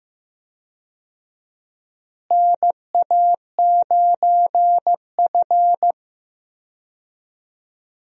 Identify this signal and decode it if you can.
Morse code, heard as a keyed tone, about 15 wpm, 700 Hz. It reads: NA9F
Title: USA Call Signs - 4 Characters - Repeated 2x - 15wpm